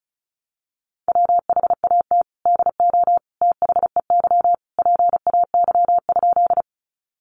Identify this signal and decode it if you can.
Morse code, heard as a keyed tone, about 34 wpm, 700 Hz. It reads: WHAT DO THEY PAY?
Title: Sentences from Top 700 Words 35wpm